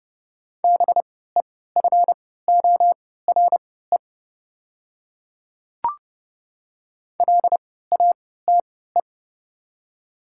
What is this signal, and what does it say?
Morse code, heard as a keyed tone, about 30 wpm, spaced as for 17 wpm, 700 Hz. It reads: BEFORE LATE
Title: 3x Character Spacing - Top 300 Words - 30wpm